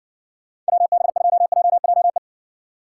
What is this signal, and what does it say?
Morse code, heard as a keyed tone, about 62 wpm, 700 Hz. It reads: WB2JJE